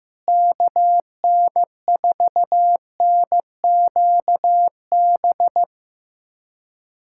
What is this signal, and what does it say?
Morse code, heard as a keyed tone, about 16 wpm, 700 Hz. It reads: KN4NQB
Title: USA Callsign DE Callsign 15wpm